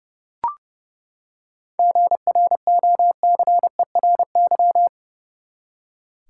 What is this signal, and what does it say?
Morse code, heard as a keyed tone, about 30 wpm, 700 Hz. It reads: GROCERY
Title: Seven Letter Words - 30wpm